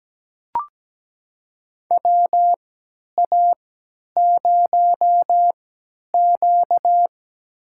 Morse code, heard as a keyed tone, about 17 wpm, spaced as for 10 wpm, 700 Hz. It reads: WA0Q
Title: USA Call Signs - 3x Character Spacing - 17wpm